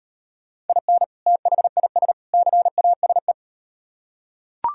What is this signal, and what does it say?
Morse code, heard as a keyed tone, about 38 wpm, 700 Hz. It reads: IN THIS CASE